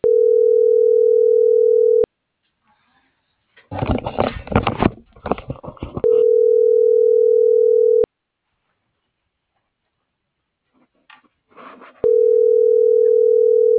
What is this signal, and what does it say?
Ringback tone (North America), 440 + 480 Hz.